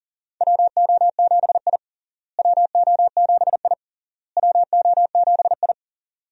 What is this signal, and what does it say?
Morse code, heard as a keyed tone, about 40 wpm, 700 Hz. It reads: WO7I WO7I WO7I
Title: USA Call Signs - 4 Characters - Repeated 3x - 40wpm